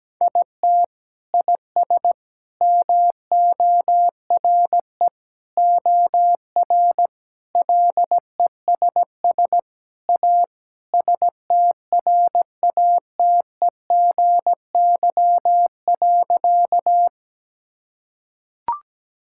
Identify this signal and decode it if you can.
Morse code, heard as a keyed tone, about 17 wpm, 700 Hz. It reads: IT IS MORE OR LESS A STRATEGY.
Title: Sentences from Top 1000 Words - Review Part 5 - 17wpm